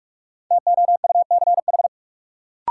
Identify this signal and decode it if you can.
Morse code, heard as a keyed tone, about 45 wpm, 700 Hz. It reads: TOUCH
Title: Sentences from Top 1200 Words - New Word Drill Alt - 45wpm